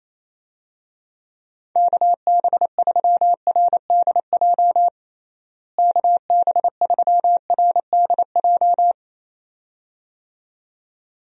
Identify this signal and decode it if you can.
Morse code, heard as a keyed tone, about 28 wpm, 700 Hz. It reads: KB3RDJ KB3RDJ
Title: USA Call Signs - 6 Characters - Repeated 2x - 28wpm